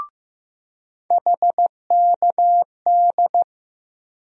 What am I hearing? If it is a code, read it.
Morse code, heard as a keyed tone, about 15 wpm, 700 Hz. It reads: HKD
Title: USA Call Sign Postfixes - 3 Letters 15wpm